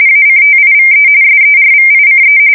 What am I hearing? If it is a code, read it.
RTTY, about 45.45 baud. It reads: ACTIVE SERVICE
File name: fsk170 / battleshipnj